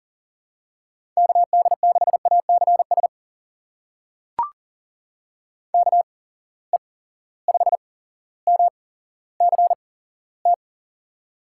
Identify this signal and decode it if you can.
Morse code, heard as a keyed tone, about 40 wpm, spaced as for 17 wpm, 700 Hz. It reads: KD6ACS KE5MCT
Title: USA Call Signs - 6 Characters - 8x Character Spacing - 40wpm